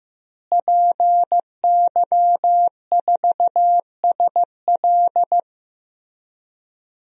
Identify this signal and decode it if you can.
Morse code, heard as a keyed tone, about 15 wpm, 700 Hz. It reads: PY4SL
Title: DXCC-100 Call Signs - 5 Characters - Repeated 3x - 15wpm